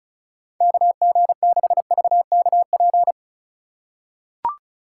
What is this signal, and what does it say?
Morse code, heard as a keyed tone, about 36 wpm, 700 Hz. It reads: KG6VKP
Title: USA Call Signs - 3x Character Spacing - 35wpm